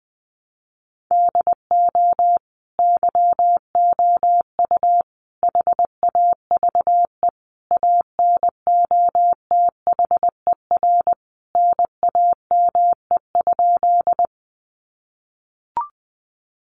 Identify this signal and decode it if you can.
Morse code, heard as a keyed tone, about 19 wpm, 700 Hz. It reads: DO YOU HAVE ANOTHER NAME?
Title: Sentences from Top 600 Words 20wpm